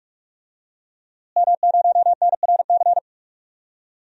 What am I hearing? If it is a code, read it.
Morse code, heard as a keyed tone, about 46 wpm, 700 Hz. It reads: M0NRC
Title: DXCC-100 Call Signs - 5 Characters - Repeated 3x - 45wpm